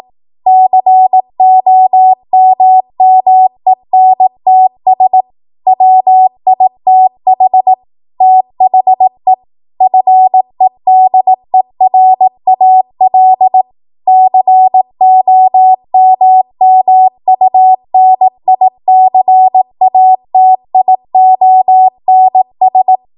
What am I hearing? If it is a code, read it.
Morse code, heard as a keyed tone, about 18 wpm, 750 Hz. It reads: COMMENTS WITH THE FEDERAL COMMUNICATIONS
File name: Morse / 18WPM